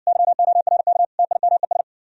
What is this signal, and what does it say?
Morse code, heard as a keyed tone, about 60 wpm, 700 Hz. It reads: YORK TIMES